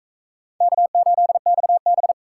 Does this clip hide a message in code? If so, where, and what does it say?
Yes: Morse code from 0.6 to 2.1 s, heard as a keyed tone, about 39 wpm, 700 Hz. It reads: K8XB